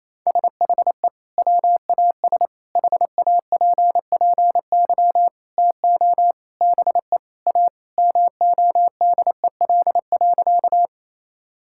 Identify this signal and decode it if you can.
Morse code, heard as a keyed tone, about 28 wpm, 700 Hz. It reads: SHE WAS HAPPY TO BE A MODEL.
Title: Sentences from Top 1000 Words - Review Part 5 - 28wpm